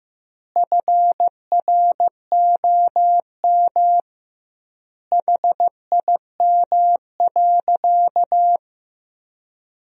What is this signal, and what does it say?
Morse code, heard as a keyed tone, about 16 wpm, 700 Hz. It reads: FROM HIM.
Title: 2x Word-Spacing - Sentences from Top 400 Words - 15wpm